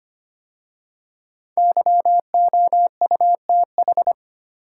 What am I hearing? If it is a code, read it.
Morse code, heard as a keyed tone, about 25 wpm, 700 Hz. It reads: YOUTH
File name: 5 Letter Words - First 3 Letters Given - 25wpm